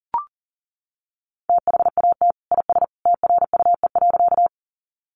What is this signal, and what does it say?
Morse code, heard as a keyed tone, about 40 wpm, 700 Hz. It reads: THAT IS TRUE.
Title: Sentences from Top 400 Words 40wpm